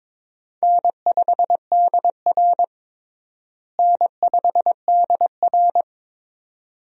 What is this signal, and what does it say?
Morse code, heard as a keyed tone, about 21 wpm, 700 Hz. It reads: N5DR N5DR
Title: USA Call Signs - 4 Characters - Repeated 2x - 22wpm